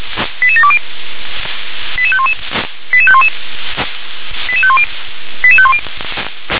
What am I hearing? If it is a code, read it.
Five-tone selective call (ZVEI), heard as quick runs of five tones, tones of about 70 ms. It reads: 88720 88511 88511 88510 88510
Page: /357/SDR/ZVEI/